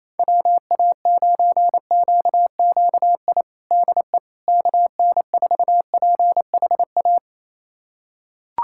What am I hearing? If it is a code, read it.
Morse code, heard as a keyed tone, about 28 wpm, 700 Hz. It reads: WA9QQI DE KN4PHA
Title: USA Callsign DE Callsign 28wpm